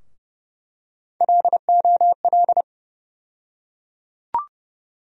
Morse code, heard as a keyed tone, about 30 wpm, 700 Hz. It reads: LOL
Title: USA Call Sign Postfixes - 3 Letters 30wpm